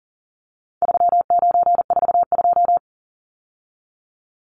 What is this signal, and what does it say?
Morse code, heard as a keyed tone, about 38 wpm, 700 Hz. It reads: 3942
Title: Numbers - 4-Digit 40wpm